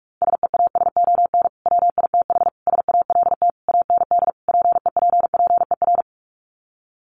Morse code, heard as a keyed tone, about 44 wpm, 700 Hz. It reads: SEASON WITH SALT AND PEPPER
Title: Sets of 5 Words 45wpm